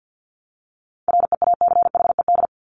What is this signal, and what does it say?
Morse code, heard as a keyed tone, about 50 wpm, 700 Hz. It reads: REACHED
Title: ED Suffix - Top 100 Words with Base Words 50wpm